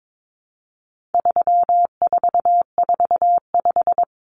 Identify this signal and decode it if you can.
Morse code, heard as a keyed tone, about 22 wpm, 700 Hz. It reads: 3445
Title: Numbers - 4-Digit 22wpm